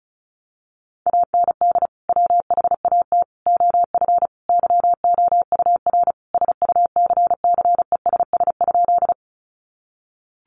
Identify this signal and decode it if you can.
Morse code, heard as a keyed tone, about 34 wpm, 700 Hz. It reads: AND WHAT OF YOUR SUCCESS?
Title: Sentences from Top 900 Words 35wpm